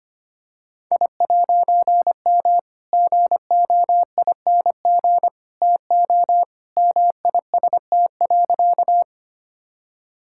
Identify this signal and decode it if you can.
Morse code, heard as a keyed tone, about 25 wpm, 700 Hz. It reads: I'M GOING TO MIST.
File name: practice / 25wpm / All-Age Appropriate Jokes - 25wpm